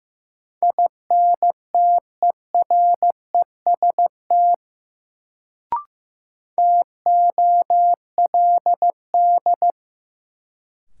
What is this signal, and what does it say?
Morse code, heard as a keyed tone, about 15 wpm, 700 Hz. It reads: INTEREST TOLD
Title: Sentences from Top 400 Words - New Word Drill - 15wpm